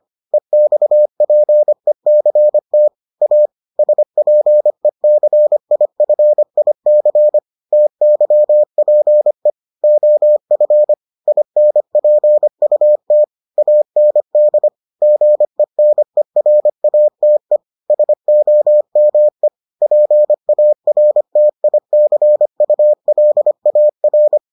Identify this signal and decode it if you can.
Morse code, heard as a keyed tone, about 25 wpm, 600 Hz. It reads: EXPECT A SPECIFIC TYPE OF INPUT AND GENERATE SOME PARTICULAR